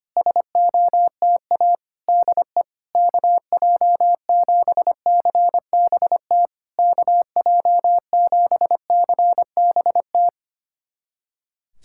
Morse code, heard as a keyed tone, about 25 wpm, 700 Hz. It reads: SOTA DE KJ7CBT KJ7CBT
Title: SOTA QSO Practice 25wpm